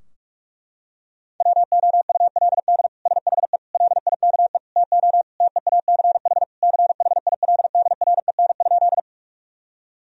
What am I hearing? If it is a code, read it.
Morse code, heard as a keyed tone, about 46 wpm, 700 Hz. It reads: WOULD SHE LIKE TO TEACH CHILDREN?
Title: Sentences from Top 1000 Words - Review Part 2 - 45wpm